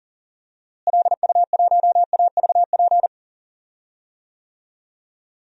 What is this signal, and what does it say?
Morse code, heard as a keyed tone, about 40 wpm, 700 Hz. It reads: LU1AVP
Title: DXCC-100 Call Signs - 6 Characters - Repeated 3x - 40wpm